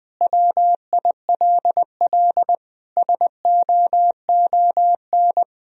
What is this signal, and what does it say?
Morse code, heard as a keyed tone, about 20 wpm, 700 Hz. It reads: WILL SOON
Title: Sentences from Top 1000 Words - Review Part 3 - 20wpm